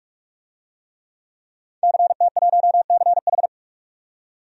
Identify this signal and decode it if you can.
Morse code, heard as a keyed tone, about 41 wpm, 690 Hz. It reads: CT1CH
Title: DXCC-100 Call Signs - 5 Characters - 45wpm